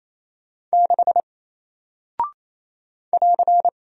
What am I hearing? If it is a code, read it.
Morse code, heard as a keyed tone, about 27 wpm, 700 Hz. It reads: 6 <AR>
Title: Single Character (Letter:Number:Punctuation:Prosign) Alt 28wpm